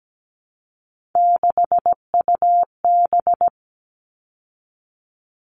Morse code, heard as a keyed tone, about 17 wpm, 700 Hz. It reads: 6UB
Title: All Cut Numbers - 3-Digit 17wpm